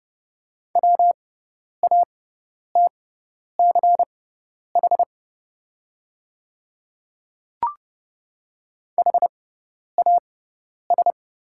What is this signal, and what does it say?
Morse code, heard as a keyed tone, about 30 wpm, spaced as for 10 wpm, 690 Hz. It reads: WATCH HAS
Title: 6x Character Spacing - Top 500 Words - 30wpm